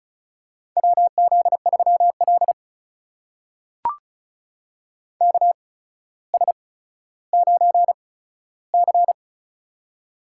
Morse code, heard as a keyed tone, about 36 wpm, spaced as for 15 wpm, 700 Hz. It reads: WZ3L KS9C
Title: USA Call Signs - 4 Characters - 8x Character Spacing - 35wpm